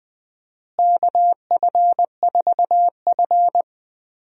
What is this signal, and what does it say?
Morse code, heard as a keyed tone, about 20 wpm, 700 Hz. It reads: KF4F
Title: USA Call Signs - 4 Characters - Repeated 3x - 20wpm